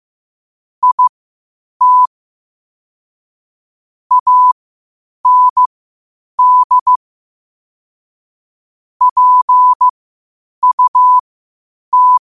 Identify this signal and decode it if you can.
Morse code, heard as a keyed tone, about 15 wpm, spaced as for 8 wpm, 1000 Hz. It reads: IT AND PUT
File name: qotdfiles / 8wpm